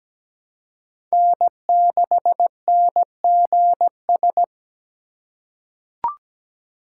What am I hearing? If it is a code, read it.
Morse code, heard as a keyed tone, about 17 wpm, 700 Hz. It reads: N6NGS
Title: USA Call Signs - 5 Characters - 4x Character Spacing - 17wpm